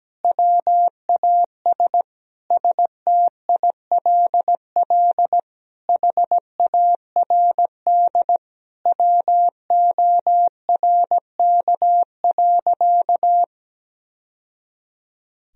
Morse code, heard as a keyed tone, about 17 wpm, 700 Hz. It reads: WAS STILL HARD WORK.
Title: Sentences from Top 1000 Words - Review Part 3 - 17wpm